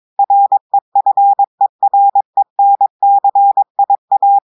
Morse code, heard as a keyed tone, about 22 wpm, 800 Hz. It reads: REFERENCIA
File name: audios / portugais / portugais texte en clair 1 vitesse 22 mots minutes